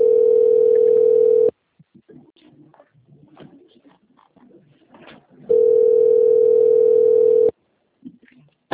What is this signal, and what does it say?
Ringback tone (North America), 440 + 480 Hz.